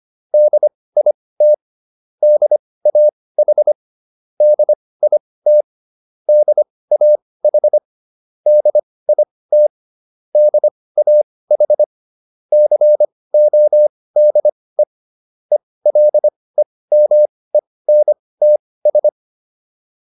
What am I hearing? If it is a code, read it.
Morse code, heard as a keyed tone, about 25 wpm, spaced as for 18 wpm, 600 Hz. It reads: DIT DAH DIT DAH DIT DAH CODE ELEMENTS